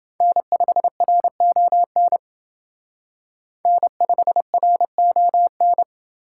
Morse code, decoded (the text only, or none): N5RON N5RON